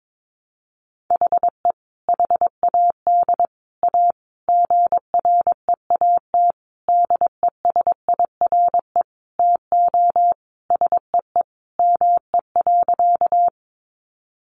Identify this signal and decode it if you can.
Morse code, heard as a keyed tone, about 22 wpm, 700 Hz. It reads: HE HAD A GREAT DESIRE TO SEE ME.
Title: Sentences from Top 1100 Words 22wpm